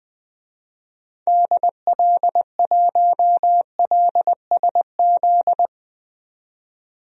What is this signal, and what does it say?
Morse code, heard as a keyed tone, about 20 wpm, 700 Hz. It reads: DL1LSZ
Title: DXCC-100 Call Signs - 6 Characters - 20wpm